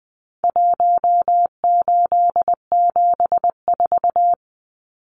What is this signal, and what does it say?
Morse code, heard as a keyed tone, about 20 wpm, 700 Hz. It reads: 1874
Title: Numbers - Contest Style - 4-Digit 20wpm